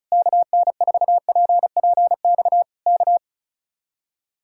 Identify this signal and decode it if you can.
Morse code, heard as a keyed tone, about 33 wpm, 700 Hz. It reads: KN4PPX K
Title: Speed-Racing Common QSO Phrases Encore - 35wpm